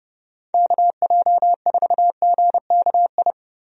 Morse code, heard as a keyed tone, about 30 wpm, 700 Hz. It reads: KJ4GKI
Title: USA Call Signs - 6 Characters - 30wpm